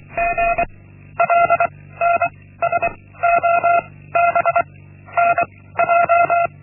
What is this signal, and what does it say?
Morse code, heard as a keyed tone, about 23 wpm, 690 Hz. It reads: G L N S O B N J
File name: Linksmakalnis-1